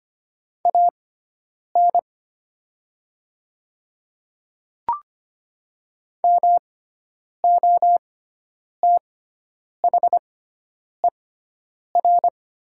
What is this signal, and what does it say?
Morse code, heard as a keyed tone, about 25 wpm, spaced as for 9 wpm, 690 Hz. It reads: AN MOTHER
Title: 6x Character Spacing - Top 200 Words - 25wpm